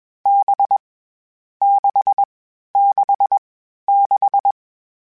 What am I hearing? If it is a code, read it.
Morse code, heard as a keyed tone, about 20 wpm, 800 Hz. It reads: B 6 6 6
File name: Difficult Combos / Difficult Combos-mp3s / NDB NDB6 @20